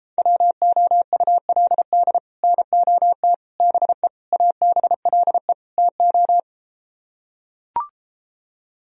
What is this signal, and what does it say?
Morse code, heard as a keyed tone, about 34 wpm, 700 Hz. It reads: WOULD NOT BE ABLE TO